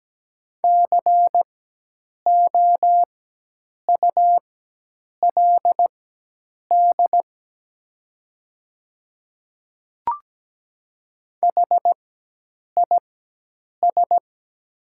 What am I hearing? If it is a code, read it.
Morse code, heard as a keyed tone, about 17 wpm, spaced as for 8 wpm, 700 Hz. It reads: COULD HIS
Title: 4x Character Spacing - Top 100 Words - 17wpm